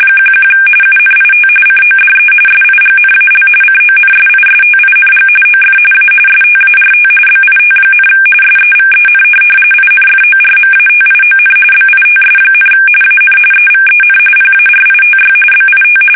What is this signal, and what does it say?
RTTY, about 45.45 baud. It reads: 11 UNIT CODE SIGNAL CONSISTING OF A START BIT, SEVEN INFORMATION BITS, AN EVEN PARITY BIT
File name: fsk850 / teletype-corp